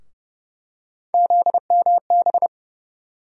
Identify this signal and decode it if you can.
Morse code, heard as a keyed tone, about 30 wpm, 700 Hz. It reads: ZMB
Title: USA Call Sign Postfixes - 3 Letters 30wpm